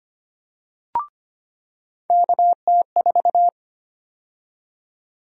Morse code, heard as a keyed tone, about 25 wpm, 700 Hz. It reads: KT4
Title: USA Call Sign Prefixes - 25wpm